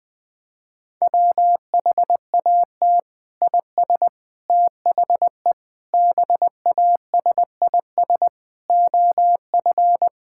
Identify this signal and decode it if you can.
Morse code, heard as a keyed tone, about 20 wpm, 700 Hz. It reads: WHAT IS THE BASIS OF
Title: Sentences from Top 1000 Words - Review Part 5 - 20wpm